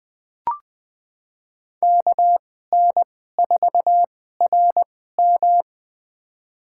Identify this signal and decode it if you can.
Morse code, heard as a keyed tone, about 20 wpm, spaced as for 15 wpm, 700 Hz. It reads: KN4RM
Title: USA Call Signs - 5 Characters - 2x Character Spacing - 20wpm